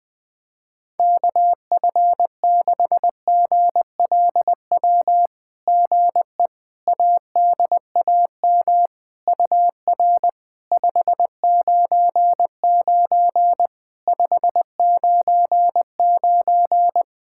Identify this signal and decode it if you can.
Morse code, heard as a keyed tone, about 20 wpm, 700 Hz. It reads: KF6GLW GE ADAM UR 599 599
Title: SOTA QSO Practice 20wpm